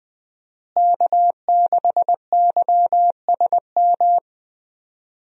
Morse code, heard as a keyed tone, about 21 wpm, 700 Hz. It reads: K6YSM
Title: USA Call Signs - 5 Characters - Repeated 3x - 20wpm